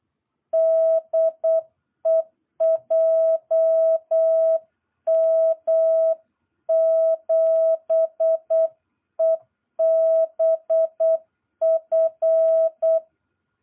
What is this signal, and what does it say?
Morse code, heard as a keyed tone, about 8 wpm, 660 Hz. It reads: DE JM7EBF
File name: jh8cbh / 2022cw